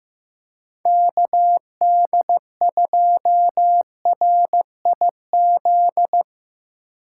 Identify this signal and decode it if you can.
Morse code, heard as a keyed tone, about 17 wpm, 700 Hz. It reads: KD2RIZ
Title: Speed Racing - USA 2x3 Call Signs 15wpm